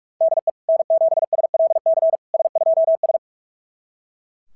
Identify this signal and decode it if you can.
Morse code, heard as a keyed tone, about 44 wpm, 660 Hz. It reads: DE N7SLC S2S
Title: SOTA QSO Practice 45wpm